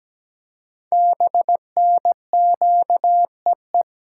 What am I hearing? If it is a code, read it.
Morse code, heard as a keyed tone, about 17 wpm, 700 Hz. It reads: BNQEE
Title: Code Groups - 5 Letters - 17wpm